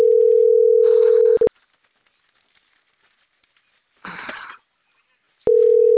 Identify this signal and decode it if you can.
Ringback tone (North America), 440 + 480 Hz.